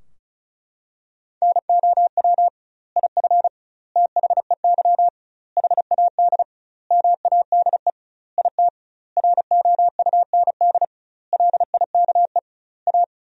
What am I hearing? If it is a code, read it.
Morse code, heard as a keyed tone, about 36 wpm, 700 Hz. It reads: NOW IF THEY HAD MADE IT ROUND LIKE A
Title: 2x Word-Spacing - Sentences from Top 500 Words - 35wpm